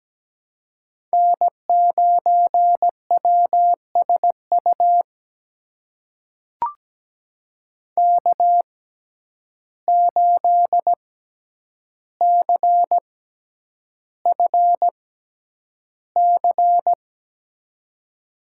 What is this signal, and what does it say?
Morse code, heard as a keyed tone, about 17 wpm, spaced as for 9 wpm, 700 Hz. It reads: N9WSU K8CFC
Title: USA Call Signs - 5 Characters - 6x Character Spacing - 17wpm